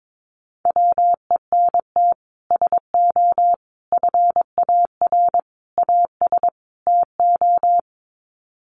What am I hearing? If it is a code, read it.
Morse code, heard as a keyed tone, about 22 wpm, 700 Hz. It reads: WENT SO FAR AS TO
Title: Sets of 5 Words Encore 22wpm